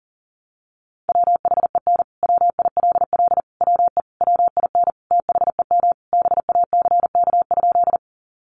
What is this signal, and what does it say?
Morse code, heard as a keyed tone, about 37 wpm, 700 Hz. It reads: WHEN WILL WE WIN THEM BACK?
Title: Sentences from Top 1500 Words 40wpm